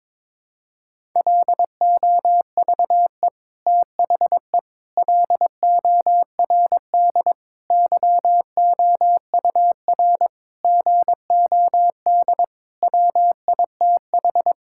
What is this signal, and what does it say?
Morse code, heard as a keyed tone, about 22 wpm, 700 Hz. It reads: LOVE THE LORD YOUR GOD WITH
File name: Sets of 9 Words - 22wpm